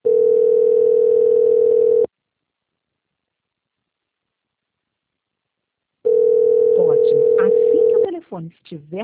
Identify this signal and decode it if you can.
Ringback tone (North America), 440 + 480 Hz, with speech.
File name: out- -2002- - - .24351